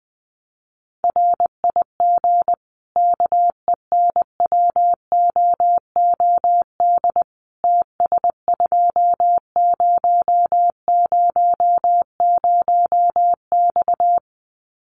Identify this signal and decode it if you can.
Morse code, heard as a keyed tone, about 20 wpm, 700 Hz. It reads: RIG KENWOOD TS2000X
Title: Common QSO Phrases 20wpm